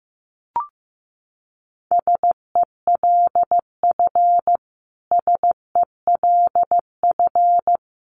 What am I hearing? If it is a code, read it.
Morse code, heard as a keyed tone, about 15 wpm, 700 Hz. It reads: SELF SELF
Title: Top 200 Words Repeated 15wpm